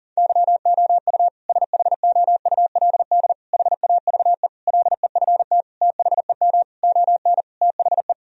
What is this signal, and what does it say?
Morse code, heard as a keyed tone, about 40 wpm, 700 Hz. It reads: YOU SHOULD HAVE LEFT THEM ON THE
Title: Sentences from Top 1000 Words - Review Part 1 - 40wpm